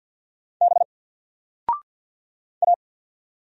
Morse code, heard as a keyed tone, about 48 wpm, 700 Hz. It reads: B A
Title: Single Character (Letter:Number:Punctuation:Prosign) Alt 50wpm